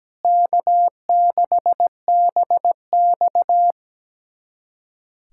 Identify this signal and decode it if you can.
Morse code, heard as a keyed tone, about 17 wpm, 700 Hz. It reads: K6BX
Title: USA Callsign DE Callsign - 4 Characters - 17wpm